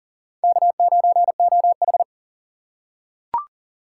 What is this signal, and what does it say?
Morse code, heard as a keyed tone, about 40 wpm, 690 Hz. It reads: K9OH
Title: USA Call Signs - 4 Characters - 6x Character Spacing - 40wpm